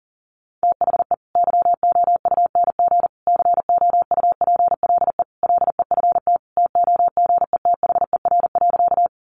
Morse code, heard as a keyed tone, about 40 wpm, 700 Hz. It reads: THE YOUNG COUPLE LEFT TOGETHER.
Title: Sentences from Top 1100 Words 40wpm